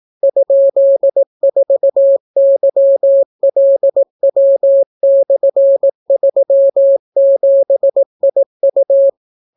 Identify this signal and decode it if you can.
Morse code, heard as a keyed tone, about 18 wpm, 550 Hz. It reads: ?4YLW/37IU